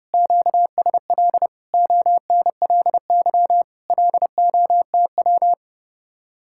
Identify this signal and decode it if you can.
Morse code, heard as a keyed tone, about 35 wpm, 700 Hz. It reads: QSL ONLY LOTW
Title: Speed-Racing Common QSO Phrases - 30wpm